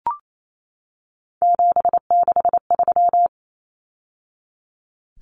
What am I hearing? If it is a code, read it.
Morse code, heard as a keyed tone, about 28 wpm, 700 Hz. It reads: 763
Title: Numbers - 3-Digit 28wpm